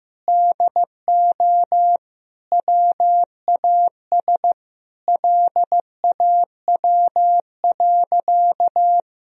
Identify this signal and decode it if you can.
Morse code, heard as a keyed tone, about 15 wpm, 700 Hz. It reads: DO WAS LAW.
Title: Sentences from Top 1000 Words - Review Part 3 - 15wpm